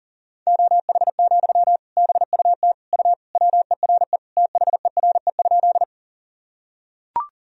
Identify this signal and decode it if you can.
Morse code, heard as a keyed tone, about 40 wpm, 700 Hz. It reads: OH, BUT U WERE THERE?
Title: Sentences from Top 500 Words and Abbreviations 40wpm